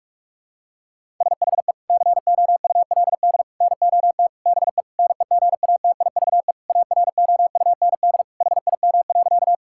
Morse code, heard as a keyed tone, about 46 wpm, 700 Hz. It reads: SHE COULD NOT BE NEGATIVE AROUND HIM.
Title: Sentences from Top 1000 Words - Review Part 6 - 45wpm